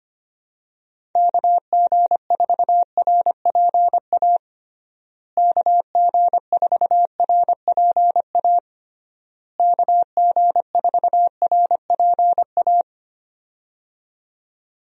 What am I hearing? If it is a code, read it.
Morse code, heard as a keyed tone, about 25 wpm, 700 Hz. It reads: KG4RPA KG4RPA KG4RPA
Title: USA Call Signs - 6 Characters - Repeated 3x - 25wpm